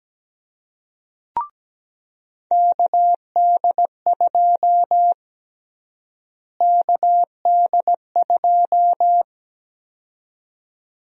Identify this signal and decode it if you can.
Morse code, heard as a keyed tone, about 17 wpm, 700 Hz. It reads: KD2 KD2
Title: USA Call Sign Prefixes - Repeated 2x - 17wpm